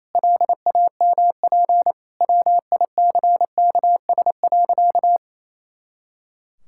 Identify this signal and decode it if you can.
Morse code, heard as a keyed tone, about 27 wpm, 700 Hz. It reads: LAMP WICKS.
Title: Pangram Challenge v2 - 28wpm